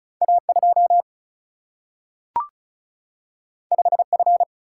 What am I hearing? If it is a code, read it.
Morse code, heard as a keyed tone, about 36 wpm, 700 Hz. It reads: A2 5F
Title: Single-Letter Single-Number 35wpm